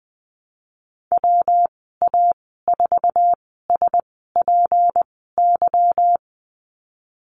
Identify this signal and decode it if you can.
Morse code, heard as a keyed tone, about 19 wpm, spaced as for 14 wpm, 700 Hz. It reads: WA4SPY
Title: 173 Lesson 28g - Number 9 - Call Signs 20wpm